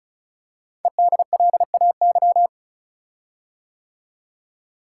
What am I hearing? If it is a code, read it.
Morse code, heard as a keyed tone, about 36 wpm, 700 Hz. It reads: EDLAY
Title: Code Groups - 5 Letters - 35wpm